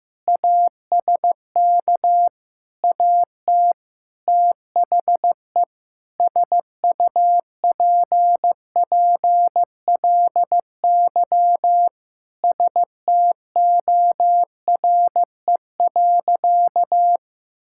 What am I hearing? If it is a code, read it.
Morse code, heard as a keyed tone, about 15 wpm, 700 Hz. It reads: ASK AT THE SUPPLY STORE.
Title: Sentences from Top 1000 Words - Review Part 5 - 15wpm